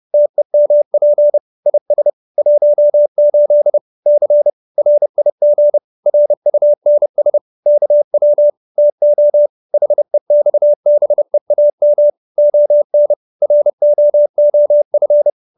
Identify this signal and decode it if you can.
Morse code, heard as a keyed tone, about 30 wpm, 590 Hz. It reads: TEMP IS 18 C RIG RUNS KW TO HEXBEAM ON ROOF
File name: QSO310 30